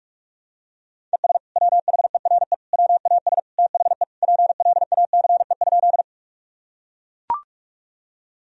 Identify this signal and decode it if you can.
Morse code, heard as a keyed tone, about 41 wpm, 700 Hz. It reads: ES WHERE WAS THE PLACE?
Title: Sentences from Top 200 Words and Abbreviations 45wpm